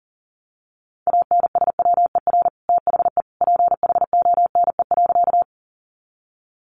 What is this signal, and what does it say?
Morse code, heard as a keyed tone, about 40 wpm, 700 Hz. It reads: ANSWER THE PHONE.
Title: Sentences from Top 1100 Words 40wpm